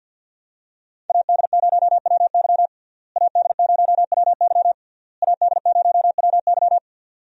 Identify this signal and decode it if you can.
Morse code, heard as a keyed tone, about 48 wpm, 700 Hz. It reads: AD0WY AD0WY AD0WY
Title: USA Call Signs - 5 Characters - Repeated 3x - 50wpm